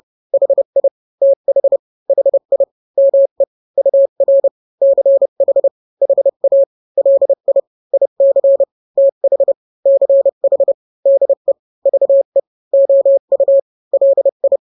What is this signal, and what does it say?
Morse code, heard as a keyed tone, about 30 wpm, 550 Hz. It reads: HI TH HI ME UR CH HA LI IC TH CH DE VE OU LI
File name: n-grams / 2-grams 30 25 WPM